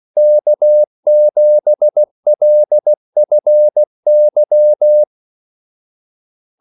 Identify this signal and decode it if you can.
Morse code, heard as a keyed tone, about 16 wpm, 600 Hz. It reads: K7LFY
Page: Kaldesignaler 16 wpm